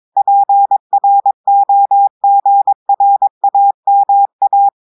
Morse code, heard as a keyed tone, about 22 wpm, 800 Hz. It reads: PROGRAMA
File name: audios / portugais / portugais texte en clair 2 vitesse 22 mots minutes